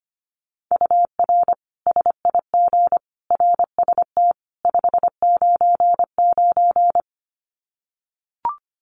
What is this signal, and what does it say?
Morse code, heard as a keyed tone, about 25 wpm, 700 Hz. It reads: UR SIG RST 599
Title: Common QSO Phrases 25wpm